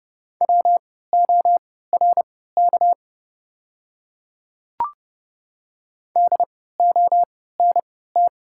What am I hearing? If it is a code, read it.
Morse code, heard as a keyed tone, about 31 wpm, spaced as for 17 wpm, 690 Hz. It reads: WORK DONT
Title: 3x Character Spacing - Top 300 Words - 30wpm